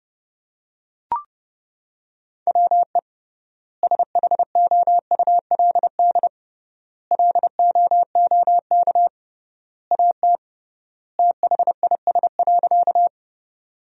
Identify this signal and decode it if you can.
Morse code, heard as a keyed tone, about 30 wpm, 700 Hz. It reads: WE SHOULD LOOK AT THIS.
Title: 3x Word-Spacing - Sentences from Top 300 Words - 30wpm